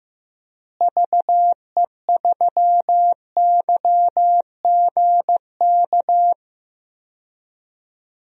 Morse code, heard as a keyed tone, about 15 wpm, 700 Hz. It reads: VE3YGK
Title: DXCC-100 Call Signs - 6 Characters - Repeated 3x - 15wpm